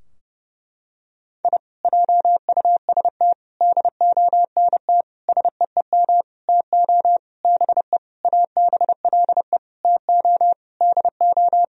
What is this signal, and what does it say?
Morse code, heard as a keyed tone, about 29 wpm, 700 Hz. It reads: I JUST DONT SEEM TO BE ABLE TO DO
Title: Sentences from Top 1000 Words - Review Part 2 - 30wpm